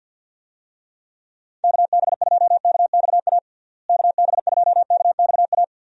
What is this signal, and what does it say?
Morse code, heard as a keyed tone, about 46 wpm, 700 Hz. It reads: KB2KXA KB2KXA
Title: USA Call Signs - 6 Characters - Repeated 3x - 50wpm